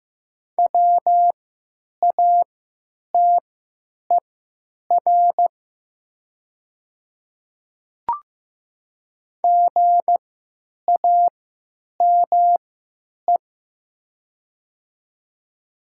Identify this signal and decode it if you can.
Morse code, heard as a keyed tone, about 15 wpm, spaced as for 8 wpm, 700 Hz. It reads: WATER GAME
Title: 3x Character Spacing - Top 500 Words - 15wpm